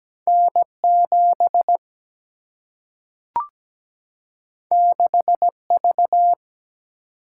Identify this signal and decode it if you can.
Morse code, heard as a keyed tone, about 17 wpm, 700 Hz. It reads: N 7 6 V
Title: Single-Letter Single-Number 17wpm